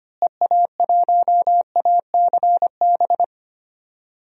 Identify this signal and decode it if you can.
Morse code, heard as a keyed tone, about 25 wpm, 700 Hz. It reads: EA1ACB
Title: DXCC-100 Call Signs - 6 Characters - Repeated 3x - 25wpm